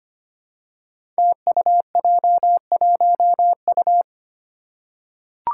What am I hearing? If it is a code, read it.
Morse code, heard as a keyed tone, about 25 wpm, 700 Hz. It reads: TUJ1U
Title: Five Challenging Letter-Number Combinations 25wpm